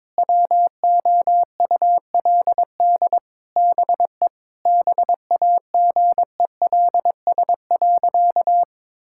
Morse code, heard as a keyed tone, about 22 wpm, 700 Hz. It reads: WOULD BE BAGELS.
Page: morse-code-ninja-practice-22wpm